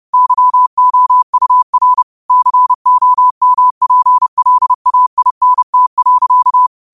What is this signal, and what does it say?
Morse code, heard as a keyed tone, about 30 wpm, 1000 Hz. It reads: YOUR COMPLAINT.
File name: 30wpm